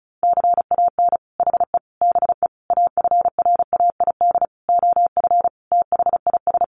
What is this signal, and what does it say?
Morse code, heard as a keyed tone, about 34 wpm, 700 Hz. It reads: CAN HE BE AFRAID OF THIS
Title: Sentences from Top 1500 Words 35wpm